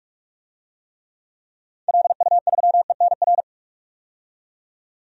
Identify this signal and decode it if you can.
Morse code, heard as a keyed tone, about 46 wpm, 700 Hz. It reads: LU3ENR
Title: DXCC-100 Call Signs - 6 Characters - 45wpm